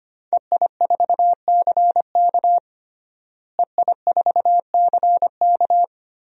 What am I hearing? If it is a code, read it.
Morse code, heard as a keyed tone, about 25 wpm, 700 Hz. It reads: EI4CK EI4CK
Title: DXCC-100 Call Signs - 5 Characters - Repeated 3x - 25wpm